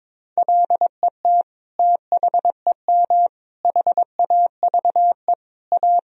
Morse code, heard as a keyed tone, about 22 wpm, 700 Hz. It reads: LET THEM HAVE A
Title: Sentences from Top 1000 Words - Review Part 3 - 22wpm